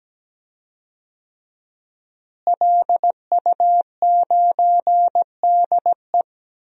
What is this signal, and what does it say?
Morse code, heard as a keyed tone, about 17 wpm, 700 Hz. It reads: LU9DE
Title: DXCC-100 Call Signs - 5 Characters - Repeated 3x - 17wpm